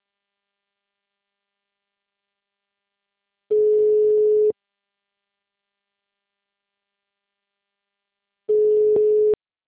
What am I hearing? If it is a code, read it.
Ringback tone (Europe, 425 Hz).
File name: out- -2030- - - .36384